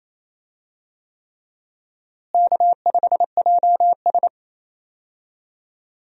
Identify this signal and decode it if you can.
Morse code, heard as a keyed tone, about 28 wpm, 700 Hz. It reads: K5JS
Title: USA Call Signs - 4 Characters - Repeated 3x - 28wpm